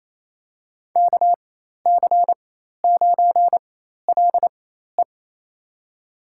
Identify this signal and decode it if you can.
Morse code, heard as a keyed tone, about 28 wpm, spaced as for 13 wpm, 690 Hz. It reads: KC9LE
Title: USA Call Signs - 5 Characters - 4x Character Spacing - 28wpm